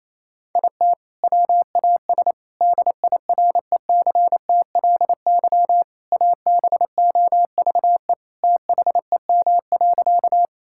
Morse code, heard as a keyed tone, about 28 wpm, 700 Hz. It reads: IT WAS DIRECTLY ABOVE THEM.
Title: Sentences from Top 1000 Words - Review Part 5 - 28wpm